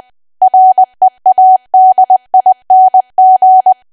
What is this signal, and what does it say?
Morse code, heard as a keyed tone, about 20 wpm, 750 Hz. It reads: READING
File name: cwa / W1AW-Level3 / 20WPM